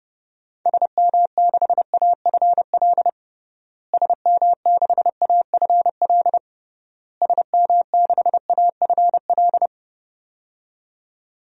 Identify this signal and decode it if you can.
Morse code, heard as a keyed tone, about 30 wpm, 700 Hz. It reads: SM6AFL SM6AFL SM6AFL
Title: DXCC-100 Call Signs - 6 Characters - Repeated 3x - 30wpm